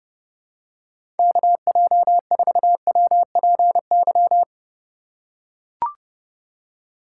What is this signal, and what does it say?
Morse code, heard as a keyed tone, about 32 wpm, 700 Hz. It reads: KJ4WPY
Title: USA Call Signs - 6 Characters - 4x Character Spacing - 30wpm